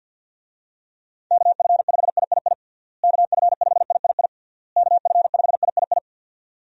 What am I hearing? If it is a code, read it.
Morse code, heard as a keyed tone, about 51 wpm, 700 Hz. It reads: KF5III KF5III KF5III
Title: USA Call Signs - 6 Characters - Repeated 3x - 50wpm